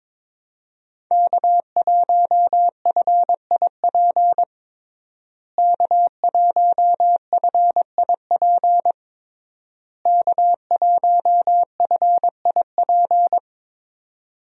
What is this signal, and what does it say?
Morse code, heard as a keyed tone, about 22 wpm, 700 Hz. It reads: K1FIP K1FIP K1FIP
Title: USA Call Signs - 5 Characters - Repeated 3x - 22wpm